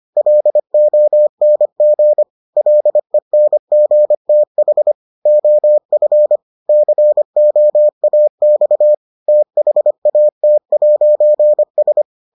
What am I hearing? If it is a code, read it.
Morse code, heard as a keyed tone, about 25 wpm, 600 Hz. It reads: LONG LENGTH OF COAX THAT'S